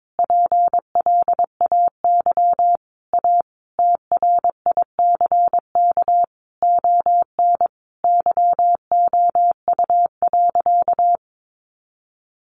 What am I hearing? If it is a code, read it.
Morse code, heard as a keyed tone, about 22 wpm, 700 Hz. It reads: PLAY A TRICK ON YOU.
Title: Sentences from Top 1500 Words 22wpm